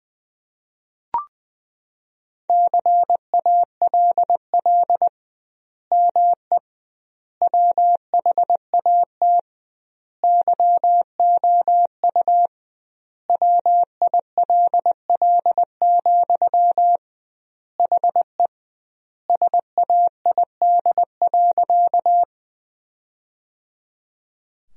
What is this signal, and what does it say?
Morse code, heard as a keyed tone, about 20 wpm, 700 Hz. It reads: CALL ME WHAT YOU WILL, HE SAID.
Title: 2x Word-Spacing - Sentences from Top 200 Words - 20wpm